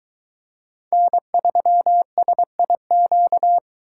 Morse code, heard as a keyed tone, about 21 wpm, 700 Hz. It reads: N3SIQ